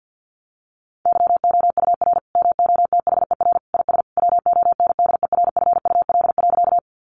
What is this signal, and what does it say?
Morse code, heard as a keyed tone, about 48 wpm, 700 Hz. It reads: YOUR MOTHER IS WONDERFUL.
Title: Sentences from Top 1100 Words 50wpm